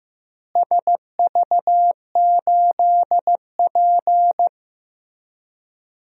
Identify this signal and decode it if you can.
Morse code, heard as a keyed tone, about 15 wpm, 700 Hz. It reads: SV8P
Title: DXCC-100 Call Signs - 4 Characters - Repeated 2x - 15wpm